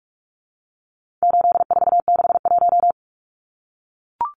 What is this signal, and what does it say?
Morse code, heard as a keyed tone, about 44 wpm, 700 Hz. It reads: 8461
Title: Numbers - 4-Digit 45wpm